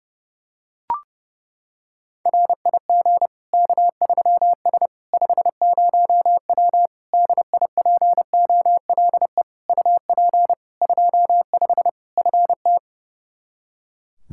Morse code, heard as a keyed tone, about 30 wpm, 700 Hz. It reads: RIG K3S 50W DIPOLE UP 25 FT
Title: Practice QSO - Exchange 2 - Info - 30wpm